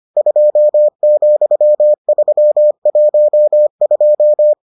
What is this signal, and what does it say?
Morse code, heard as a keyed tone, about 25 wpm, 600 Hz. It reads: 2,312